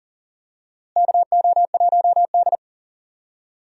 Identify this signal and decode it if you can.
Morse code, heard as a keyed tone, about 39 wpm, 700 Hz. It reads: KO1D
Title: USA Call Signs - 4 Characters - Repeated 2x - 40wpm